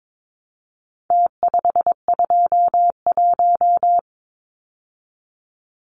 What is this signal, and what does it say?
Morse code, heard as a keyed tone, about 22 wpm, 700 Hz. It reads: T521
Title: Numbers - Contest Style - 4-Digit 22wpm